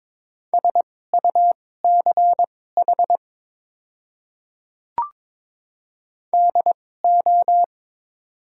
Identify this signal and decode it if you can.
Morse code, heard as a keyed tone, about 21 wpm, spaced as for 16 wpm, 700 Hz. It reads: SUCH DO
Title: 2x Character Spacing - Top 200 Words - 22wpm